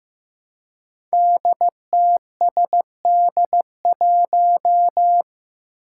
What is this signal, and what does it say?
Morse code, heard as a keyed tone, about 15 wpm, 700 Hz. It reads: DTSD1
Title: Five Challenging Letter-Number Combinations 15wpm